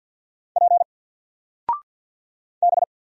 Morse code, heard as a keyed tone, about 50 wpm, 700 Hz. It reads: P B
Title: Single Letter-Number 50wpm